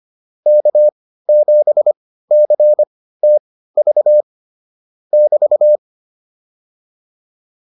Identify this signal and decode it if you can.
Morse code, heard as a keyed tone, about 25 wpm, spaced as for 15 wpm, 600 Hz. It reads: K7CTV <BT>